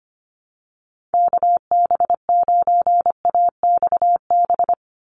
Morse code, heard as a keyed tone, about 25 wpm, 700 Hz. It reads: KB9AXB
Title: USA Call Signs 25wpm